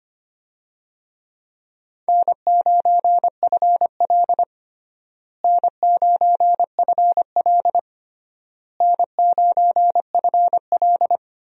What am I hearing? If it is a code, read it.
Morse code, heard as a keyed tone, about 25 wpm, 700 Hz. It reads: N9FL N9FL N9FL
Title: USA Call Signs - 4 Characters - Repeated 3x - 25wpm